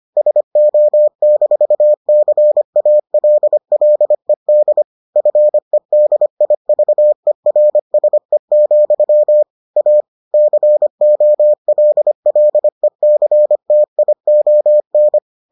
Morse code, heard as a keyed tone, about 25 wpm, 600 Hz. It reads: SO-CALLED FEDIVERSE, A COLLECTION